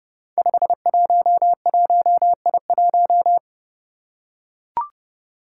Morse code, heard as a keyed tone, about 28 wpm, 700 Hz. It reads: 511I1
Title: Five Challenging Letter-Number Combinations 30wpm